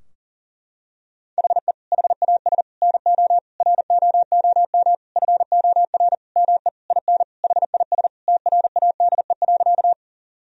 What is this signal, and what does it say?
Morse code, heard as a keyed tone, about 40 wpm, 690 Hz. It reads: HE HAS NO ROOM FOR ME IN HIS TRADE.
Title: Sentences from Top 1000 Words - Review Part 4 - 40wpm